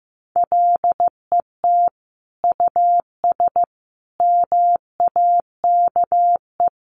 Morse code, heard as a keyed tone, about 15 wpm, 700 Hz. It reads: LET US MAKE
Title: Sentences from Top 700 Words 15wpm